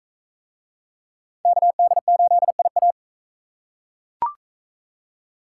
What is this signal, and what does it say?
Morse code, heard as a keyed tone, about 40 wpm, 700 Hz. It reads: KD8IA